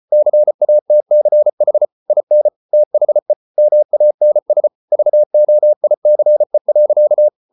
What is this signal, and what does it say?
Morse code, heard as a keyed tone, about 33 wpm, 600 Hz. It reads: CATCH IN THE MANS VOICE.